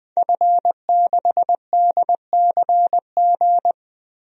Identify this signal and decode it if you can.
Morse code, heard as a keyed tone, about 20 wpm, 700 Hz. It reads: F6DCG
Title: DXCC-100 Call Signs - 5 Characters - Repeated 2x - 20wpm